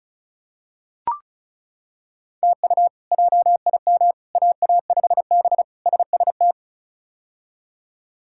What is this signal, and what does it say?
Morse code, heard as a keyed tone, about 36 wpm, 700 Hz. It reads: TU JIM AA5B SST
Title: K1USN SST QSO Practice - 35wpm